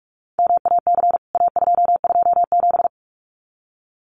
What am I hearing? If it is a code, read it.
Morse code, heard as a keyed tone, about 44 wpm, 700 Hz. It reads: MAC A227
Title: CWT Exchange 45wpm